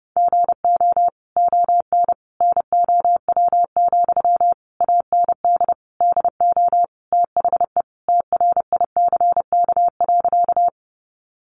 Morse code, heard as a keyed tone, about 30 wpm, 700 Hz. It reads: GO ON NOW, AND DO THE TRICK.
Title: Sentences from Top 1500 Words 30wpm